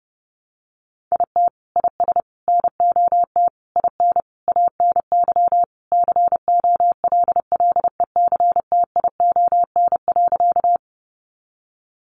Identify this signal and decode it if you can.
Morse code, heard as a keyed tone, about 30 wpm, 700 Hz. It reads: IT IS NOT IN ANY COLLECTION.
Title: Sentences from Top 1200 Words 30wpm